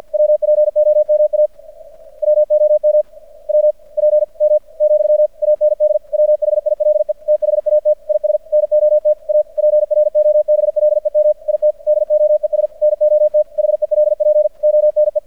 Morse code, heard as a keyed tone, about 50 wpm, 600 Hz. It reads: WPOMA WOM WPM, AND WHILE THAT IS NOT A PROBLEM IT DOES NOT HELP ONE
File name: archived / CRSnet